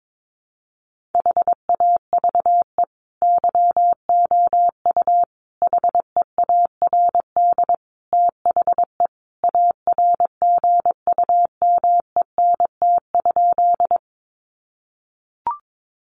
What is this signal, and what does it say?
Morse code, heard as a keyed tone, about 22 wpm, 700 Hz. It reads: HAVE YOU HEARD THE ARGUMENT?
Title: Sentences from Top 1400 Words 22wpm